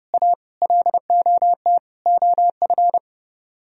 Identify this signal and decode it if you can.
Morse code, heard as a keyed tone, about 30 wpm, 700 Hz. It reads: A LOT OF